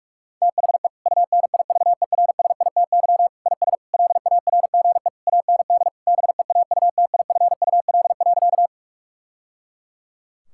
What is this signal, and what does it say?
Morse code, heard as a keyed tone, about 42 wpm, 700 Hz. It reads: THE UNIVERSITY IS LARGE AND BEAUTIFUL.
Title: Sentences from Top 1000 Words - Review Part 5 - 45wpm